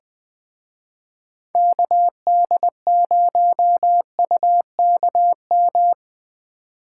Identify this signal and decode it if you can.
Morse code, heard as a keyed tone, about 20 wpm, 700 Hz. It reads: KD0UKM
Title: Speed Racing - USA 2x3 Call Signs 17wpm